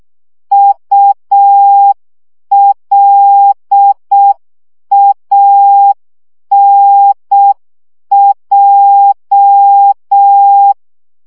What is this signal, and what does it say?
Morse code, heard as a keyed tone, about 6 wpm, 800 Hz. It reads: ULANJ
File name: morze vizsga 2011-1 5